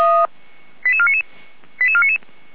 Five-tone selective call (ZVEI), heard as quick runs of five tones, tones of about 70 ms. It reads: 88499 88499
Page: ZVEI